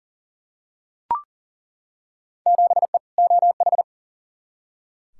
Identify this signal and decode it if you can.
Morse code, heard as a keyed tone, about 40 wpm, 700 Hz. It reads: 7E OH
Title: Field Day Exchanges Encore - 40wpm